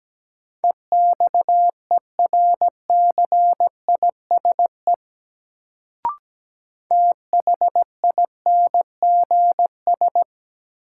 Morse code, heard as a keyed tone, about 17 wpm, 700 Hz. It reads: EXERCISE THINGS
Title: Sentences from Top 1000 Words - Review Part 4 - New Word Drill - 17wpm